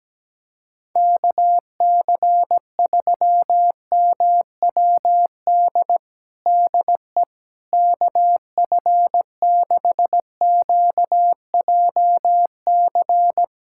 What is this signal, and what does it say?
Morse code, heard as a keyed tone, about 17 wpm, 700 Hz. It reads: KC3MWD DE KF6QJC
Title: USA Callsign DE Callsign 17wpm